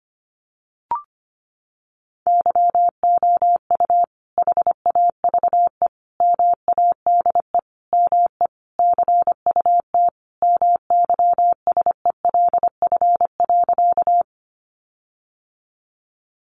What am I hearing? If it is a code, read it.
Morse code, heard as a keyed tone, about 25 wpm, 700 Hz. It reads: YOU HAVE MADE ME CUT MYSELF.
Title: Sentences from Top 1300 Words 25wpm